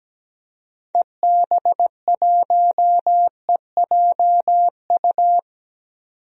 Morse code, heard as a keyed tone, about 17 wpm, 700 Hz. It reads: EB1EJU
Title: DXCC-100 Call Signs - 6 Characters - 17wpm